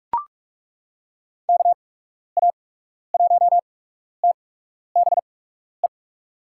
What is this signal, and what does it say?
Morse code, heard as a keyed tone, about 41 wpm, spaced as for 12 wpm, 700 Hz. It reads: KA1TBE
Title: USA Call Signs - 6 Characters - 8x Character Spacing - 45wpm